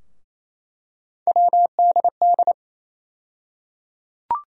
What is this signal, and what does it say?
Morse code, heard as a keyed tone, about 28 wpm, 700 Hz. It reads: WDD
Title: USA Call Sign Postfixes - 3 Letters 28wpm